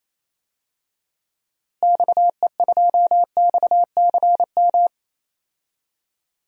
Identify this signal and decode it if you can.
Morse code, heard as a keyed tone, about 28 wpm, 700 Hz. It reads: XE2XCM
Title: DXCC-100 Call Signs - 6 Characters - Repeated 3x - 28wpm